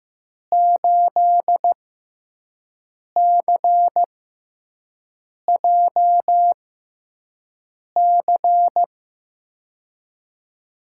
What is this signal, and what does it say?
Morse code, heard as a keyed tone, about 15 wpm, spaced as for 5 wpm, 700 Hz. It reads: 8CJC
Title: USA Call Signs - 5 Characters - 6x Character Spacing - 15wpm